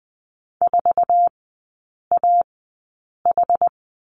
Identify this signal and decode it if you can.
Morse code, heard as a keyed tone, about 20 wpm, spaced as for 8 wpm, 700 Hz. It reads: 4AH
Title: 060 Lesson 11e - Letter h - Three Characters 20wpm